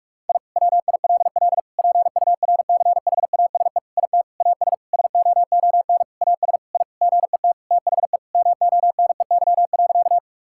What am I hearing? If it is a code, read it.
Morse code, heard as a keyed tone, about 46 wpm, 700 Hz. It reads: I WILL PURCHASE IT AS SOON AS I GET THE MONEY.
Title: Sentences from Top 1000 Words - Review Part 5 - 45wpm